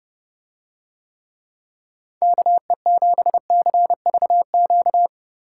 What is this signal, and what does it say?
Morse code, heard as a keyed tone, about 30 wpm, 700 Hz. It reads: KE7CVQ